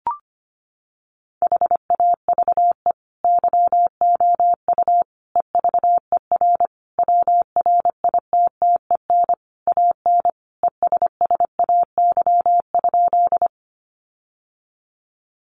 Morse code, heard as a keyed tone, about 25 wpm, 700 Hz. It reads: HAVE YOU EVER WRITTEN AN ESSAY?
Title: Sentences from Top 1500 Words 25wpm